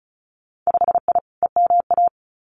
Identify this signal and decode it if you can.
Morse code, heard as a keyed tone, about 36 wpm, 700 Hz. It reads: 5I EMA
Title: Winter Field Day Exchanges Encore 35wpm